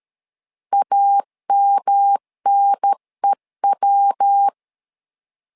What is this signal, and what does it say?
Morse code, heard as a keyed tone, about 13 wpm, 800 Hz. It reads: AMNEW
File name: LEZIONI / 3 MWRO / 3 MW 12wpm 1633